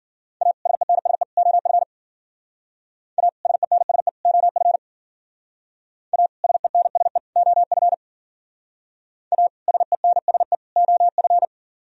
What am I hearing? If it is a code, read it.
Morse code, heard as a keyed tone, about 52 wpm, 700 Hz. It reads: A SENSE OF A SENSE OF A SENSE OF A SENSE OF